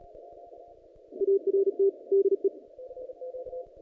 Morse code, heard as a keyed tone, about 38 wpm, 390 Hz. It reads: ART DE
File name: contests / naqp-cw-k5tr-op-k5pi- -0300